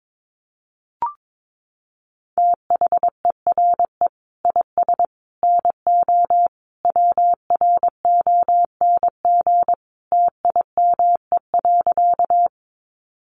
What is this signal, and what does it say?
Morse code, heard as a keyed tone, about 22 wpm, 700 Hz. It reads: THERE IS NO WRONG TIME.
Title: Sentences from Top 900 Words 22wpm